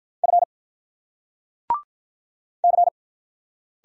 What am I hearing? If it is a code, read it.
Morse code, heard as a keyed tone, about 47 wpm, 700 Hz. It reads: F C